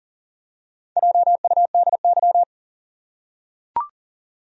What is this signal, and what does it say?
Morse code, heard as a keyed tone, about 37 wpm, 700 Hz. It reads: JUDY
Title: USA Common Names - Max 5 Letters - 40wpm